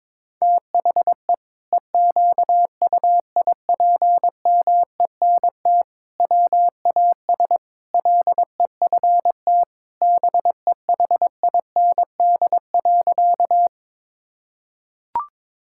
Morse code, heard as a keyed tone, about 22 wpm, 700 Hz. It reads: THE EQUIPMENT WAS LEFT BEHIND.
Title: Sentences from Top 1000 Words - Review Part 5 - 22wpm